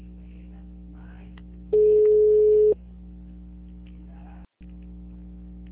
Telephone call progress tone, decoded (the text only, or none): ringback tone